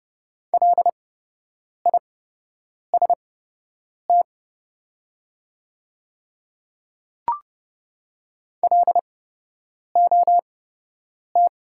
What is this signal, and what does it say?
Morse code, heard as a keyed tone, about 30 wpm, spaced as for 8 wpm, 690 Hz. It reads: LIST LOT
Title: 8x Character Spacing - Top 500 Words - 30wpm